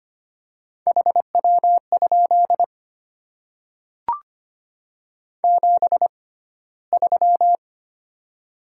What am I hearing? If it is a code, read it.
Morse code, heard as a keyed tone, about 25 wpm, spaced as for 15 wpm, 700 Hz. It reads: HW? 73
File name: Top 20 QSO Elements - 6x Character Spacing - 25wpm